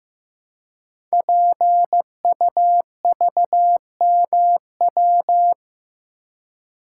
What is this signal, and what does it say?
Morse code, heard as a keyed tone, about 15 wpm, 700 Hz. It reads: PUVMW
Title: Code Groups - 5 Letters - 15wpm